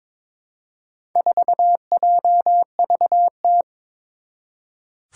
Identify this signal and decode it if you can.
Morse code, heard as a keyed tone, about 22 wpm, 700 Hz. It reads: 4JVT
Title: Four Challenging Letter-Number Combinations 22wpm